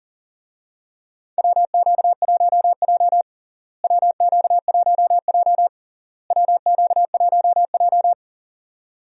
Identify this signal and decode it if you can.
Morse code, heard as a keyed tone, about 40 wpm, 700 Hz. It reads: WQ1J WQ1J WQ1J
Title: USA Call Signs - 4 Characters - Repeated 3x - 40wpm